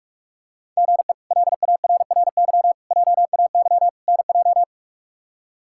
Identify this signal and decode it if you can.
Morse code, heard as a keyed tone, about 46 wpm, 700 Hz. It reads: GE LARRY JAY NJ
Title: K1USN SST QSO Practice - 45wpm